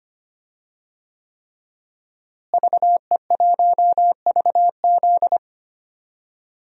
Morse code, heard as a keyed tone, about 25 wpm, 700 Hz. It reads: VE1VZ
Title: DXCC-100 Call Signs - 5 Characters - Repeated 3x - 25wpm